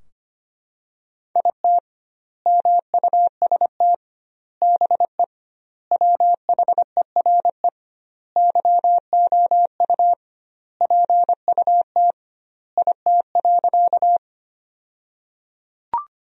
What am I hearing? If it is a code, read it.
Morse code, heard as a keyed tone, about 25 wpm, 700 Hz. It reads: IT MUST BE WHERE YOU PUT IT.
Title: 2x Word-Spacing - Sentences from Top 200 Words - 25wpm